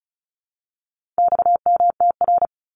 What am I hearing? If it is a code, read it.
Morse code, heard as a keyed tone, about 34 wpm, 700 Hz. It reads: XMTR
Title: Abbreviations 35wpm